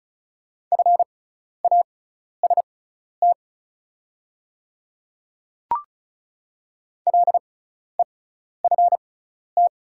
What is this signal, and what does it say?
Morse code, heard as a keyed tone, about 36 wpm, spaced as for 12 wpm, 700 Hz. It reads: FAST LEFT
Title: 6x Character Spacing - Top 500 Words - 35wpm